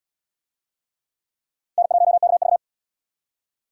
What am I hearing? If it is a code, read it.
Morse code, heard as a keyed tone, about 77 wpm, 700 Hz. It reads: N1KV